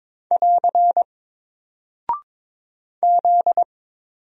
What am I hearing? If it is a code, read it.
Morse code, heard as a keyed tone, about 22 wpm, 700 Hz. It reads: <AR> Z
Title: Single Character (Letter:Number:Punctuation:Prosign) 22wpm